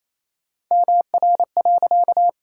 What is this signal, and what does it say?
Morse code, heard as a keyed tone, about 28 wpm, 700 Hz. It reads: MR.
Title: Pangram Challenge v2 - 28wpm